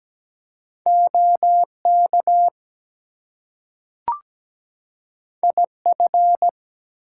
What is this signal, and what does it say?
Morse code, heard as a keyed tone, about 17 wpm, 700 Hz. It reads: OK IF
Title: Two Letter Words - 17wpm